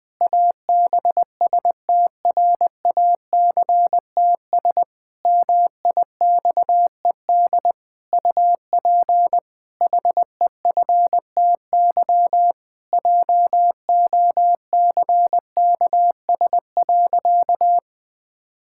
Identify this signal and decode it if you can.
Morse code, heard as a keyed tone, about 21 wpm, 700 Hz. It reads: ABSTRACTS MIXED UP HEFTY JOCKS.
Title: Pangram Challenge v2 - 20wpm